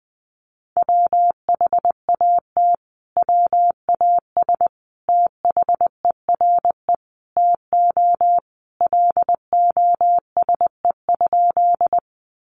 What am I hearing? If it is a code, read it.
Morse code, heard as a keyed tone, about 20 wpm, 700 Hz. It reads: WHAT WAS THERE TO LOSE?
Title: Sentences from Top 1100 Words 20wpm